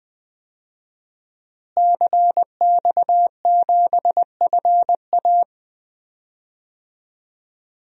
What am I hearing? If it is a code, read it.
Morse code, heard as a keyed tone, about 21 wpm, 700 Hz. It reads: CX7FA
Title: DXCC-100 Call Signs - 5 Characters - Repeated 3x - 20wpm